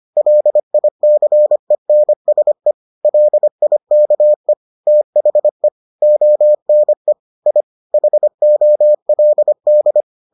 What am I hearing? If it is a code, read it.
Morse code, heard as a keyed tone, about 25 wpm, 600 Hz. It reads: LICENSE LIKE THE ONE I HOLD